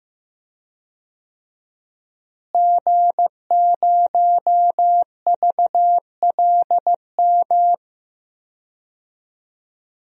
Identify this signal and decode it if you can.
Morse code, heard as a keyed tone, about 15 wpm, 700 Hz. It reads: G0VLM
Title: DXCC-100 Call Signs - 5 Characters - Repeated 2x - 15wpm